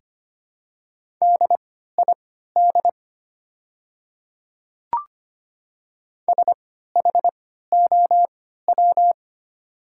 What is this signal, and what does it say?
Morse code, heard as a keyed tone, about 25 wpm, spaced as for 14 wpm, 690 Hz. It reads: DID SHOW
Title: 3x Character Spacing - Top 200 Words - 25wpm